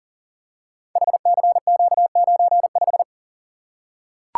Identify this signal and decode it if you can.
Morse code, heard as a keyed tone, about 39 wpm, 690 Hz. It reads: HCQ95
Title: Code Groups - 5 Letter:Numbers - 40wpm